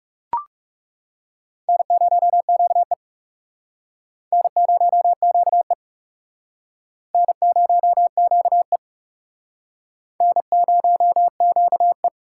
Morse code, heard as a keyed tone, about 38 wpm, 700 Hz. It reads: N0QE N0QE N0QE N0QE